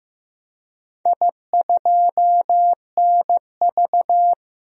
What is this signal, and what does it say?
Morse code, heard as a keyed tone, about 15 wpm, 700 Hz. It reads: I2NV
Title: DXCC-100 Call Signs - 4 Characters - 15wpm